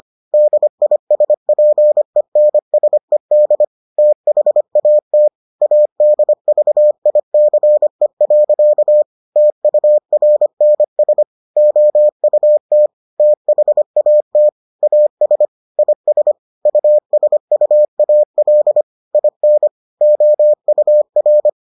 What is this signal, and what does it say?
Morse code, heard as a keyed tone, about 25 wpm, 600 Hz. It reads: DISPENSED THAT ADVICE. TURNS OUT THAT AS IS USUAL IN OUR